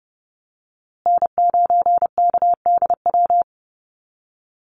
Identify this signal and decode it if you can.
Morse code, heard as a keyed tone, about 30 wpm, 700 Hz. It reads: N9KDW
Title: USA Call Signs Repeated 3x 30wpm